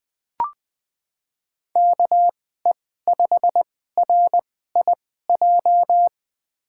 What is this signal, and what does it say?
Morse code, heard as a keyed tone, about 20 wpm, spaced as for 14 wpm, 700 Hz. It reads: KE5RIJ
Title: USA Call Signs - 6 Characters - 2x Character Spacing - 20wpm